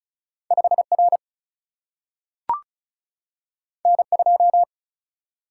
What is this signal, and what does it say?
Morse code, heard as a keyed tone, about 36 wpm, 700 Hz. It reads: 5 R N 2
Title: Single-Letter Single-Number 35wpm